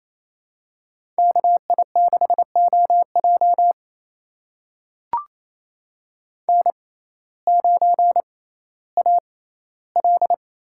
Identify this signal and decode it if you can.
Morse code, heard as a keyed tone, about 28 wpm, spaced as for 15 wpm, 700 Hz. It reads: KI6OJ N9AL
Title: USA Call Signs - 5 Characters - 6x Character Spacing - 28wpm